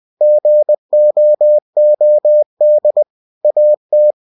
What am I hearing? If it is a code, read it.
Morse code, heard as a keyed tone, about 20 wpm, 600 Hz. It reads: GOOD AT
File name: cw-books / winnie-the-pooh / winnie-20wpm-page-0045